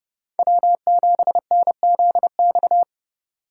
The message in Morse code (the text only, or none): W7NZX